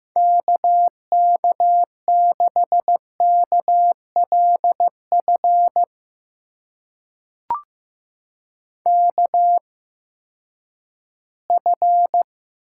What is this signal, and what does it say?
Morse code, heard as a keyed tone, about 15 wpm, spaced as for 10 wpm, 700 Hz. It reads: KK6KLF KF
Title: USA Call Signs - 6 Characters - 8x Character Spacing - 15wpm